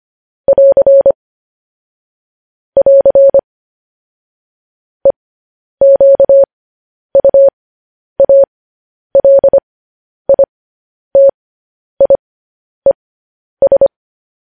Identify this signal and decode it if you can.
Morse code, heard as a keyed tone, about 25 wpm, spaced as for 10 wpm, 550 Hz. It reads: <AR> <AR> EQUALITIES
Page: news headlines in morse code at 10 wpm